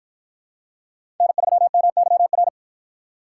Morse code, heard as a keyed tone, about 53 wpm, 700 Hz. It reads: N3MYR